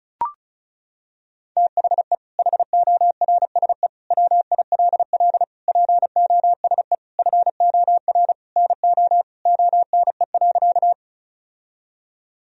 Morse code, heard as a keyed tone, about 34 wpm, 700 Hz. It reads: THE HORSE WILL POSE FOR NO ONE.
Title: Sentences from Top 1000 Words - Review Part 2 - 35wpm